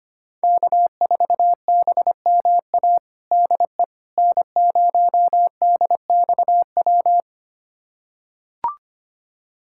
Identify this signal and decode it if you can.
Morse code, heard as a keyed tone, about 25 wpm, 700 Hz. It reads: K4BMA DE N0DXW
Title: USA Callsign DE Callsign - 5 Characters - 25wpm